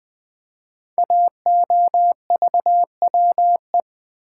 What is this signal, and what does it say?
Morse code, heard as a keyed tone, about 20 wpm, 700 Hz. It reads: AOVWE
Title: Code Groups - 5 Letters - 20wpm